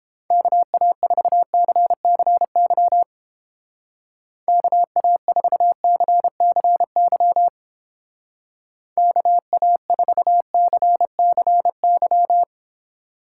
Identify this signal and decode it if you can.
Morse code, heard as a keyed tone, about 29 wpm, 700 Hz. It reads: KA4CCY KA4CCY KA4CCY